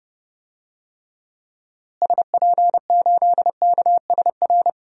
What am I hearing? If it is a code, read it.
Morse code, heard as a keyed tone, about 29 wpm, 700 Hz. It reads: SP8KSR
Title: DXCC-100 Call Signs - 6 Characters - Repeated 3x - 30wpm